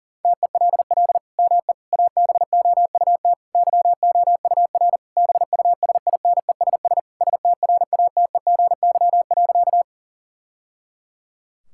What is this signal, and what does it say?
Morse code, heard as a keyed tone, about 40 wpm, 700 Hz. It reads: TELL ME ABOUT YOUR BUSINESS STRATEGY.
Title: Sentences from Top 1000 Words - Review Part 5 - 40wpm